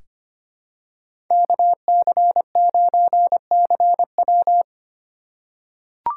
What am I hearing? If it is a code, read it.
Morse code, heard as a keyed tone, about 25 wpm, 700 Hz. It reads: KC9CW
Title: USA Call Signs - 5 Characters - 3x Character Spacing - 25wpm